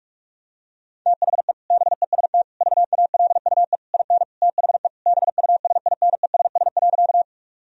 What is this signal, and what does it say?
Morse code, heard as a keyed tone, about 41 wpm, 700 Hz. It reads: THE BEST VALUE IN THE BUSINESS.
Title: Sentences from Top 1000 Words - Review Part 3 - 45wpm